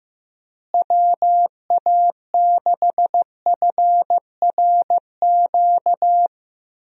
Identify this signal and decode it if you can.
Morse code, heard as a keyed tone, about 15 wpm, 700 Hz. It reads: WA6FRQ
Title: USA Call Signs - 6 Characters - Repeated 2x - 15wpm